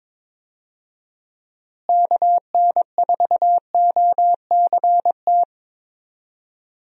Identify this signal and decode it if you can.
Morse code, heard as a keyed tone, about 21 wpm, 700 Hz. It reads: KN4OCT
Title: USA Call Signs - 6 Characters - Repeated 3x - 22wpm